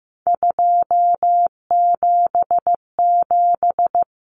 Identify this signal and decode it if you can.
Morse code, heard as a keyed tone, about 15 wpm, 700 Hz. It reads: 277
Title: Common Cut Numbers - 3-Digit 15wpm